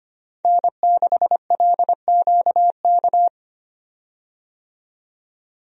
Morse code, heard as a keyed tone, about 25 wpm, 690 Hz. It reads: N6LQK
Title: USA Call Signs - 5 Characters - 25wpm